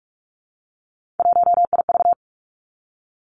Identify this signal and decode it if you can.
Morse code, heard as a keyed tone, about 46 wpm, 700 Hz. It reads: 1IV
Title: Three Challenging Letter-Number Combinations 45wpm